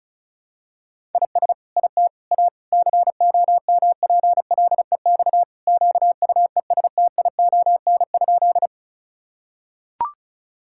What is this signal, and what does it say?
Morse code, heard as a keyed tone, about 34 wpm, 700 Hz. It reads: IS IT A COMPLEX QUESTION?
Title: Sentences from Top 1000 Words - Review Part 6 - 35wpm